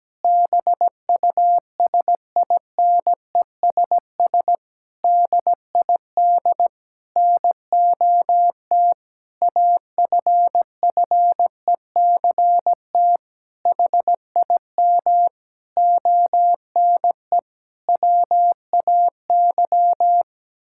Morse code, heard as a keyed tone, about 17 wpm, 700 Hz. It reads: BUSINESS DID NOT AFFECT HIM ONE WAY
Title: Sentences from Top 1000 Words - Review Part 6 - 17wpm